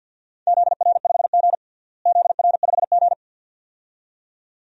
Morse code, heard as a keyed tone, about 53 wpm, 700 Hz. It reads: ZR5G ZR5G
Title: DXCC-100 Call Signs - 4 Characters - Repeated 2x - 50wpm